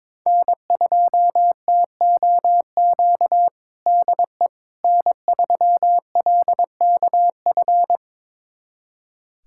Morse code, heard as a keyed tone, about 22 wpm, 700 Hz. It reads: N2TOQ DE N3LKF
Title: USA Callsign DE Callsign - 5 Characters - 22wpm